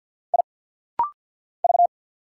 Morse code, heard as a keyed tone, about 50 wpm, 700 Hz. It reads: I V
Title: Single Letter-Number - Rapid-Fire 50wpm